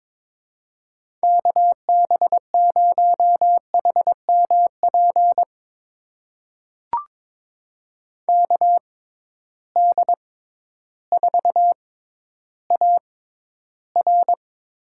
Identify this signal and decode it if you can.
Morse code, heard as a keyed tone, about 22 wpm, spaced as for 12 wpm, 700 Hz. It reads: KB0HMP KD4AR
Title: USA Call Signs - 6 Characters - 6x Character Spacing - 22wpm